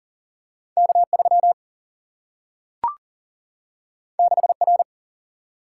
Morse code, heard as a keyed tone, about 40 wpm, 700 Hz. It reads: K3 6R
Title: Single-Letter Single-Number 40wpm